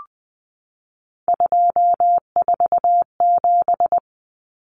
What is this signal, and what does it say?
Morse code, heard as a keyed tone, about 20 wpm, 700 Hz. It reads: 247
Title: Numbers - 3-Digit 20wpm